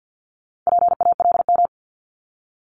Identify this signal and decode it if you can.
Morse code, heard as a keyed tone, about 50 wpm, 690 Hz. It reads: PALM
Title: Multi-letter Phonemes Repeated 3x 50wpm